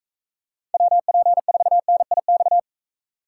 Morse code, heard as a keyed tone, about 42 wpm, 700 Hz. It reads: WP4NIX